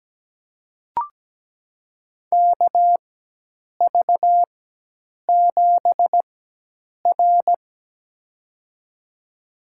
Morse code, heard as a keyed tone, about 17 wpm, spaced as for 8 wpm, 700 Hz. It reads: KV7R
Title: USA Call Signs - 4 Characters - 4x Character Spacing - 17wpm